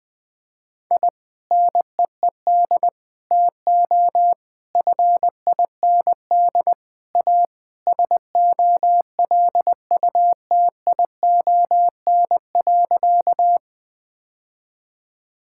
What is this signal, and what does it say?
Morse code, heard as a keyed tone, about 20 wpm, 700 Hz. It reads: I NEED TO FIND A SOLUTION.
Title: Sentences with 3-Word Phrases - 20wpm